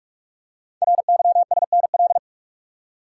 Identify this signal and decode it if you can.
Morse code, heard as a keyed tone, about 44 wpm, 700 Hz. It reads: RYSNL
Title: Code Groups - 5 Letters - 45wpm